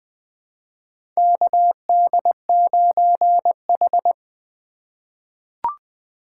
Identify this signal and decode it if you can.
Morse code, heard as a keyed tone, about 20 wpm, 700 Hz. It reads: KD9H
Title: USA Call Signs - 4x Character Spacing - 20wpm